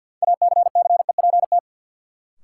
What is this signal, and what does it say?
Morse code, heard as a keyed tone, about 46 wpm, 700 Hz. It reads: ACCEPT
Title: Sentences from Top 1300 Words - New Word Drill Alt - 50wpm